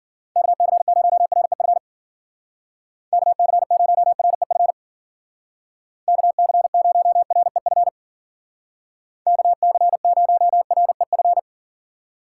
Morse code, heard as a keyed tone, about 51 wpm, 700 Hz. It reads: KC0REF KC0REF KC0REF KC0REF